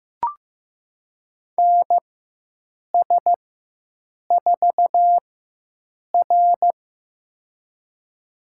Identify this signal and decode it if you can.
Morse code, heard as a keyed tone, about 15 wpm, spaced as for 7 wpm, 700 Hz. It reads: NS4R
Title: USA Call Signs - 4x Character Spacing - 15wpm